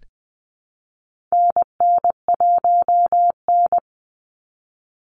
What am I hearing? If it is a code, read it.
Morse code, heard as a keyed tone, about 20 wpm, 700 Hz. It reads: NN1N
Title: Call Signs Repeated 3x - 20wpm